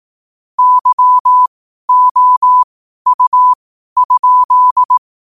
Morse code, heard as a keyed tone, about 18 wpm, spaced as for 12 wpm, 1000 Hz. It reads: YOU?
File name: qotdfiles / 12wpm